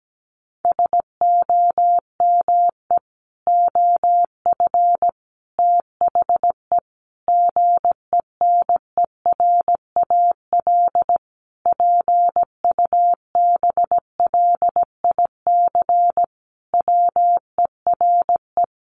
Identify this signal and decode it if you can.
Morse code, heard as a keyed tone, about 17 wpm, 700 Hz. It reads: SOME OF THE GENERAL PUBLIC WERE
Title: Sentences from Top 800 Words 17wpm